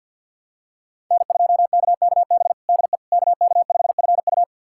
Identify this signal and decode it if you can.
Morse code, heard as a keyed tone, about 50 wpm, 700 Hz. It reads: N2KKB DE KK5FU
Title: USA Callsign DE Callsign 50wpm